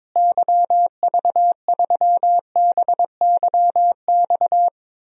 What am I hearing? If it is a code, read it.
Morse code, heard as a keyed tone, about 22 wpm, 700 Hz. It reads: YV3BYX
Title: DXCC-100 Call Signs - 6 Characters - Repeated 2x - 22wpm